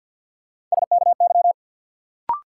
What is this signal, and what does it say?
Morse code, heard as a keyed tone, about 51 wpm, 700 Hz. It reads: SKY
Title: Sentences from Top 1200 Words - New Word Drill - 50wpm